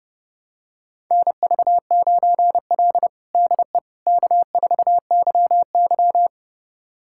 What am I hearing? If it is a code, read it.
Morse code, heard as a keyed tone, about 30 wpm, 700 Hz. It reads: NV9L DE K4YY
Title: USA Callsign DE Callsign - 4 Characters - 30wpm